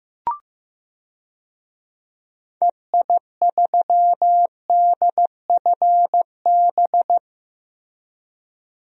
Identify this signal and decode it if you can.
Morse code, heard as a keyed tone, about 15 wpm, 700 Hz. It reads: EI3DFB
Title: DXCC-100 Call Signs - 6 Characters - Repeated 3x - 15wpm